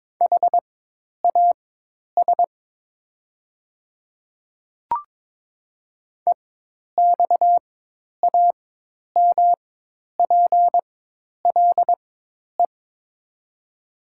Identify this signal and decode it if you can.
Morse code, heard as a keyed tone, about 22 wpm, spaced as for 10 wpm, 700 Hz. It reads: HAS EXAMPLE
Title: 4x Character Spacing - Top 300 Words - 22wpm